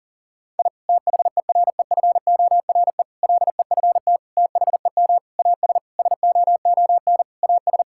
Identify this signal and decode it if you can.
Morse code, heard as a keyed tone, about 40 wpm, 700 Hz. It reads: I THEREFORE LEFT THEM AS SOON AS
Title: Sentences from Top 1000 Words - Review Part 5 - 40wpm